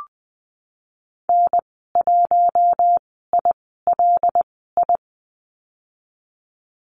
Morse code, heard as a keyed tone, about 20 wpm, spaced as for 15 wpm, 700 Hz. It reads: N1ILI
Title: 122 Lesson 20g - Number 3 - Call Signs 20wpm